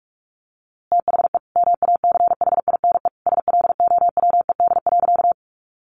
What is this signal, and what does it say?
Morse code, heard as a keyed tone, about 44 wpm, 700 Hz. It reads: THE MACHINE SLOWED.
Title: A Sound Of Thunder 45wpm